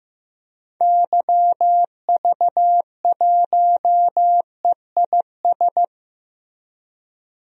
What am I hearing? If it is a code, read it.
Morse code, heard as a keyed tone, about 15 wpm, 700 Hz. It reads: YV1EIS
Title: DXCC-100 Call Signs - 6 Characters - Repeated 3x - 15wpm